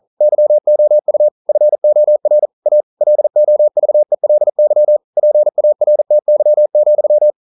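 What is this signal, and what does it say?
Morse code, heard as a keyed tone, about 42 wpm, 600 Hz. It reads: YOU FOR A LOVELY PARTY,
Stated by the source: Morse code practice at 41 wpm, from CW med Gnister.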